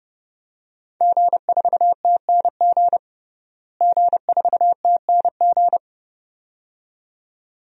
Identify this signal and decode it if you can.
Morse code, heard as a keyed tone, about 30 wpm, 700 Hz. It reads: G4TNG G4TNG
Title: DXCC-100 Call Signs - 5 Characters - Repeated 2x - 30wpm